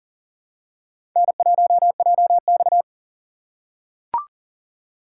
Morse code, heard as a keyed tone, about 40 wpm, 700 Hz. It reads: N1JX
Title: USA Call Signs - 4 Characters - 3x Character Spacing - 40wpm